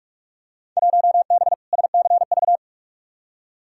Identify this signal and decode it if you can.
Morse code, heard as a keyed tone, about 45 wpm, 700 Hz. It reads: 1B SCV
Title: Field Day Exchanges - 45wpm